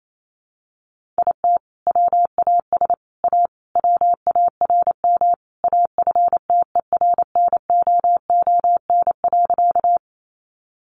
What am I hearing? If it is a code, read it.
Morse code, heard as a keyed tone, about 28 wpm, 700 Hz. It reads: IT WAS A WARM AFTERNOON.
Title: Sentences from Top 1500 Words 28wpm